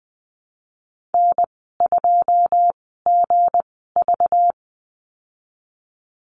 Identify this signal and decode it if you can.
Morse code, heard as a keyed tone, about 19 wpm, spaced as for 14 wpm, 700 Hz. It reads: N2GV
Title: 213 Lesson 34g - Number 8 - Call Signs 20wpm